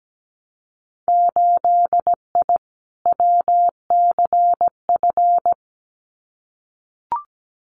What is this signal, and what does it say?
Morse code, heard as a keyed tone, about 17 wpm, 700 Hz. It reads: 8I WCF
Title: Winter Field Day Exchanges Encore 17wpm